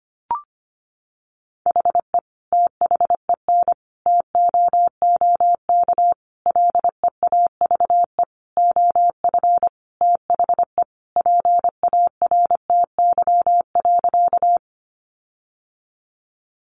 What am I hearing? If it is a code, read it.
Morse code, heard as a keyed tone, about 25 wpm, 700 Hz. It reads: HE THEN TOOK LEAVE OF THE PARTY.
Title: Sentences from Top 1400 Words 25wpm